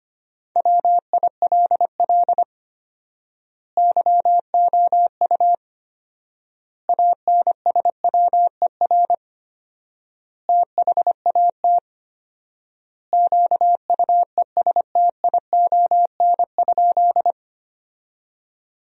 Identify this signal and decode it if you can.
Morse code, heard as a keyed tone, about 25 wpm, 700 Hz. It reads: WILL YOU ANSWER THAT QUESTION?
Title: 4x Word-Spacing - Sentences from Top 400 Words - 25wpm